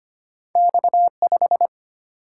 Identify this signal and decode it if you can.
Morse code, heard as a keyed tone, about 33 wpm, 700 Hz. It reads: X5